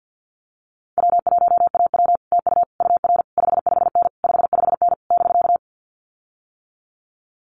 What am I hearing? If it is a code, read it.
Morse code, heard as a keyed tone, about 50 wpm, 700 Hz. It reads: W1AW TU UR 55N 55N <BK>
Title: Common QSO Phrases Encore 50wpm